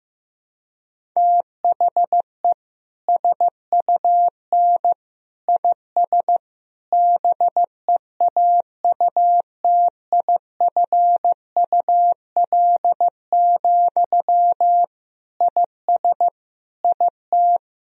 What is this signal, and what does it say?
Morse code, heard as a keyed tone, about 16 wpm, 700 Hz. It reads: THE SUN IS BEAUTIFUL, IS IT
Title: Sentences from Top 1000 Words - Review Part 1 - 15wpm